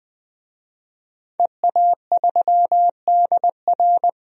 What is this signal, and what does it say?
Morse code, heard as a keyed tone, about 20 wpm, 700 Hz. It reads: EA3DR
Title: DXCC-100 Call Signs - 5 Characters - 20wpm